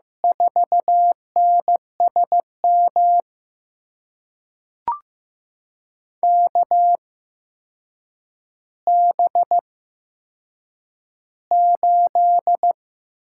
Morse code, heard as a keyed tone, about 15 wpm, spaced as for 7 wpm, 700 Hz. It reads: 4NSM KB8
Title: USA Call Signs - 6 Characters - 8x Character Spacing - 15wpm